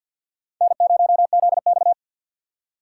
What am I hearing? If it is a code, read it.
Morse code, heard as a keyed tone, about 52 wpm, 700 Hz. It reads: N0ZX
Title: USA Call Signs - 4 Characters - 3x Character Spacing - 50wpm